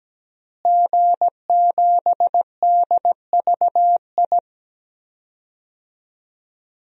Morse code, heard as a keyed tone, about 17 wpm, 700 Hz. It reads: G7DVI
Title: DXCC-100 Call Signs - 5 Characters - Repeated 3x - 17wpm